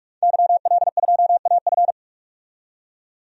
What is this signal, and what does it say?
Morse code, heard as a keyed tone, about 47 wpm, 700 Hz. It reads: YL2AF
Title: DXCC-100 Call Signs - 5 Characters - 45wpm